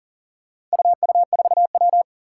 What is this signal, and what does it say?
Morse code, heard as a keyed tone, about 40 wpm, 700 Hz. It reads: UU4W
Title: Four Challenging Letter-Number Combinations 40wpm